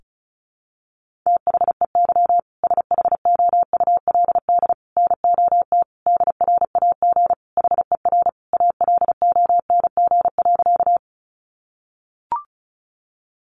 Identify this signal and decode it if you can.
Morse code, heard as a keyed tone, about 34 wpm, 700 Hz. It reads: THEY SHOULD NOT DRAG HER ALONG.
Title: Sentences from Top 1500 Words 35wpm